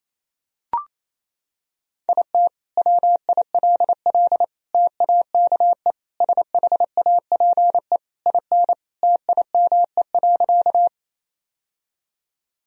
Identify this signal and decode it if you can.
Morse code, heard as a keyed tone, about 28 wpm, 700 Hz. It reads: IT WILL TAKE SHAPE IN TIME.
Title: Sentences from Top 1000 Words - Review Part 3 - 28wpm